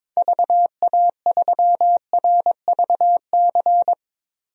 Morse code, heard as a keyed tone, about 22 wpm, 700 Hz. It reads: VA3RVC
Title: DXCC-100 Call Signs - 6 Characters - 22wpm